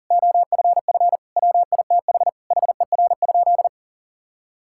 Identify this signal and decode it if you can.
Morse code, heard as a keyed tone, about 40 wpm, 700 Hz. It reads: OFF WITH HER?
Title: Sentences from Top 1000 Words - Review Part 6 - 40wpm